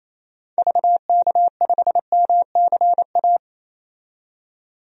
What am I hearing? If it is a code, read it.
Morse code, heard as a keyed tone, about 27 wpm, 700 Hz. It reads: VK5MCA
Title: DXCC-100 Call Signs - 6 Characters - 28wpm